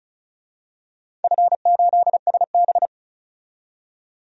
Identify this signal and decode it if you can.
Morse code, heard as a keyed tone, about 36 wpm, 690 Hz. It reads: F8SB
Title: DXCC-100 Call Signs - 4 Characters - 35wpm